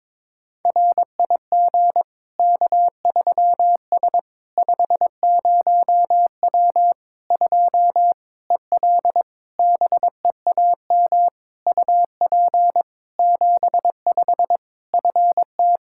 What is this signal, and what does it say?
Morse code, heard as a keyed tone, about 22 wpm, 700 Hz. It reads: RIG K3S 50W 2 EL BEAM UP 75 FT
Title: Practice QSO - Exchange 2 - Info - 22wpm